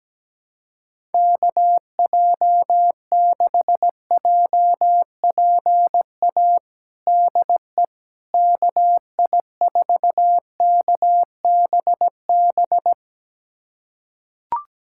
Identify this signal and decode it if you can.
Morse code, heard as a keyed tone, about 17 wpm, 700 Hz. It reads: KJ6JPA DE KI4KBB
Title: USA Callsign DE Callsign - 6 Characters - 17wpm